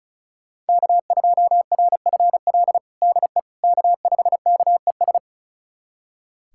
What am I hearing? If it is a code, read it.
Morse code, heard as a keyed tone, about 34 wpm, 700 Hz. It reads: K2RFL DE K5KES
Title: USA Callsign DE Callsign - 5 Characters - 35wpm